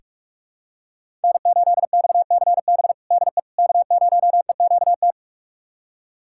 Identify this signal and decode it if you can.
Morse code, heard as a keyed tone, about 41 wpm, 700 Hz. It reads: N8XCB DE K0EQT
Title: USA Callsign DE Callsign - 5 Characters - 45wpm